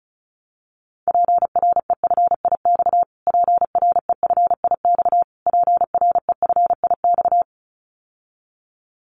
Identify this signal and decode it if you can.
Morse code, heard as a keyed tone, about 34 wpm, 700 Hz. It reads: PREFIX PREFIX PREFIX
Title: Callsign Prefixes for Top 25 DXCC - Repeated 3x - 35wpm